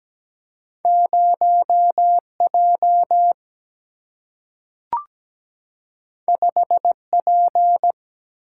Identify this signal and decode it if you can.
Morse code, heard as a keyed tone, about 17 wpm, 700 Hz. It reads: 0J 5P
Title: Single-Letter Single-Number 17wpm